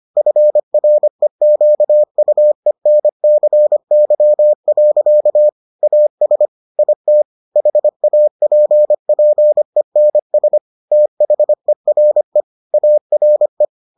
Morse code, heard as a keyed tone, about 25 wpm, 600 Hz. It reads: FREQUENCY. AS IT HAPPENS THERE ARE